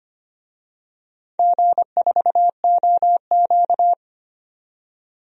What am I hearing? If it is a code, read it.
Morse code, heard as a keyed tone, about 25 wpm, 700 Hz. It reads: G4OQ
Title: DXCC-100 Call Signs - 4 Characters - 25wpm